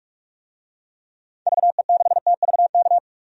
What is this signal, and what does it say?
Morse code, heard as a keyed tone, about 46 wpm, 700 Hz. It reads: VE6TVK
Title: DXCC-100 Call Signs - 6 Characters - 45wpm